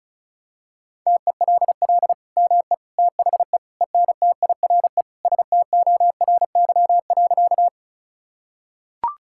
Morse code, heard as a keyed tone, about 36 wpm, 700 Hz. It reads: TELL ME THE ENTIRE STORY.
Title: Sentences from Top 1000 Words - Review Part 5 - 35wpm